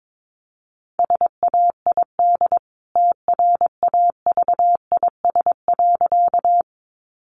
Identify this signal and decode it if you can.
Morse code, heard as a keyed tone, about 22 wpm, 700 Hz. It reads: SAID TRAVIS.
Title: A Sound Of Thunder 22wpm